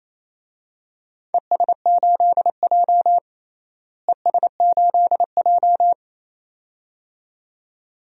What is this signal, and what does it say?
Morse code, heard as a keyed tone, about 28 wpm, 700 Hz. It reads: ES8J ES8J
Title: DXCC-100 Call Signs - 4 Characters - Repeated 2x - 28wpm